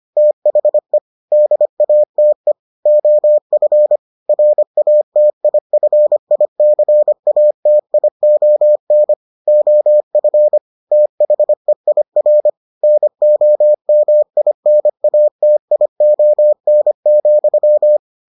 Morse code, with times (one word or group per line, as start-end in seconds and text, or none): 0.2-1.0 THE
1.3-2.5 DATE
2.9-4.0 OF
4.3-9.1 RATIFICATION
9.5-10.6 OF
10.9-12.5 THEIR
12.8-18.0 NOMINATION,